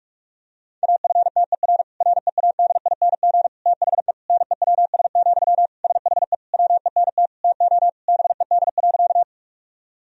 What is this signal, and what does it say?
Morse code, heard as a keyed tone, about 41 wpm, 700 Hz. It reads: AFTER READING THE NEWS, SHE WENT TO BED.
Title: Sentences from Top 1000 Words - Review Part 3 - 45wpm